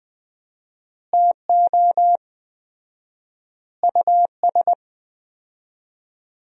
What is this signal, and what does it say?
Morse code, heard as a keyed tone, about 20 wpm, 700 Hz. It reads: TO US
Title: 4x Word-Spacing - Sentences from Top 500 Words - 20wpm